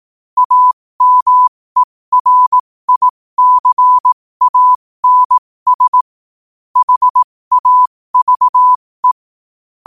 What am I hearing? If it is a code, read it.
Morse code, heard as a keyed tone, about 18 wpm, spaced as for 15 wpm, 1000 Hz. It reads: AMERICANS HAVE
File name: qotdfiles / 15wpm